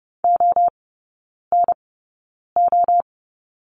Morse code, heard as a keyed tone, about 30 wpm, spaced as for 9 wpm, 700 Hz. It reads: ONO
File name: Lesson 18e - Letter m - Three Characters 30wpm